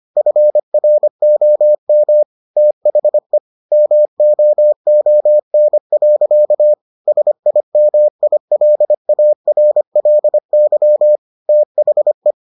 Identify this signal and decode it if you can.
Morse code, heard as a keyed tone, about 25 wpm, 600 Hz. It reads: FROM THE MOON. SIMILARLY THE